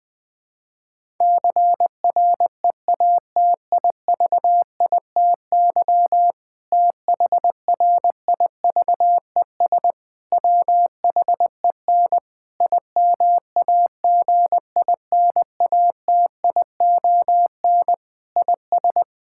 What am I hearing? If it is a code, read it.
Morse code, heard as a keyed tone, about 20 wpm, 700 Hz. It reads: CREATIVITY THRIVES WHEN IMAGINATION IS
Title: Difficult Sentences with Hints - 20wpm